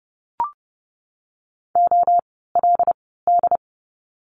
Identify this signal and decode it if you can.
Morse code, heard as a keyed tone, about 30 wpm, spaced as for 17 wpm, 700 Hz. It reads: OLD
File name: Lesson 17f - Letter u - Words 30wpm